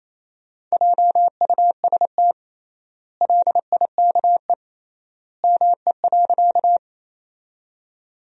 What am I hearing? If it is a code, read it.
Morse code, heard as a keyed tone, about 28 wpm, 700 Hz. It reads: JUST LIKE ME.
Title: 3x Word-Spacing - Sentences from Top 200 Words - 28wpm